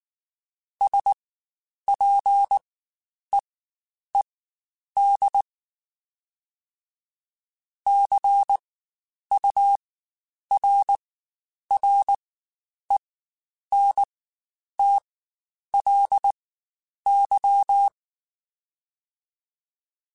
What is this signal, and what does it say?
Morse code, heard as a keyed tone, about 19 wpm, spaced as for 9 wpm, 800 Hz. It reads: SPEED CURRENTLY
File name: Spaced Characters / Spaced Characters-mp3s / Weather5 5 sp @18